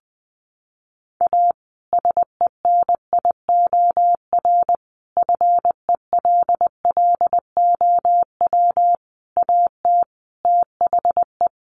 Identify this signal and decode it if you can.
Morse code, heard as a keyed tone, about 20 wpm, 700 Hz. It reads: A SENIOR FELLOW AT THE
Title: Sets of 5 Words Encore 20wpm